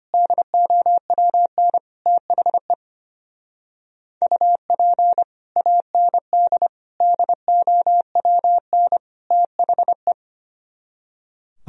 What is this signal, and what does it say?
Morse code, heard as a keyed tone, about 31 wpm, 700 Hz. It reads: DOWN THE UP AND DOWN THE
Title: Speed-Racing - Sets of 4 Words 25wpm